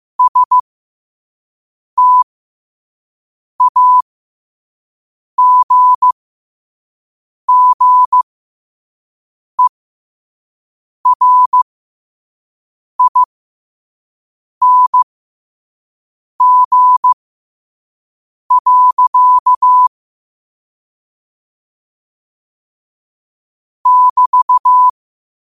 Morse code, heard as a keyed tone, about 15 wpm, spaced as for 5 wpm, 1000 Hz. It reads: STAGGERING. <BT>
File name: qotdfiles / 5wpm